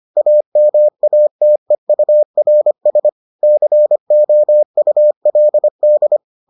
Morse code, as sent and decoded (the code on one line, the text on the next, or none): .- -- .- - . ..- .-. ... / -.-. --- ..- .-.. -..
AMATEURS COULD